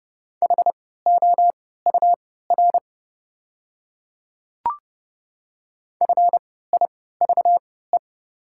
Morse code, heard as a keyed tone, about 31 wpm, spaced as for 17 wpm, 700 Hz. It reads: HOUR FIVE
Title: 3x Character Spacing - Top 500 Words - 30wpm